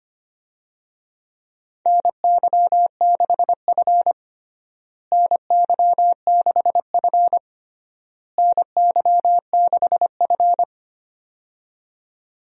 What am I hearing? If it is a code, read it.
Morse code, heard as a keyed tone, about 25 wpm, 700 Hz. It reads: NY6F NY6F NY6F
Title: USA Call Signs - 4 Characters - Repeated 3x - 25wpm